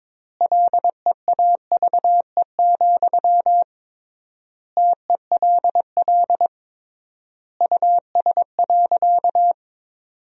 Morse code, heard as a keyed tone, about 21 wpm, 700 Hz. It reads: LEAVE, TELL US.
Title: 3x Word-Spacing - Sentences from Top 400 Words - 22wpm